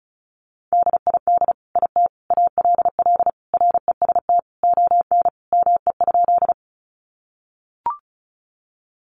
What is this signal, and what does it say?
Morse code, heard as a keyed tone, about 34 wpm, 700 Hz. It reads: DID IT ALL REST ON ME?
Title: Sentences from Top 500 Words 35wpm